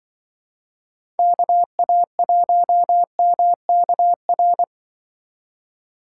Morse code, heard as a keyed tone, about 24 wpm, 700 Hz. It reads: KA1MKR
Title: Speed Racing - USA 2x3 Call Signs 20wpm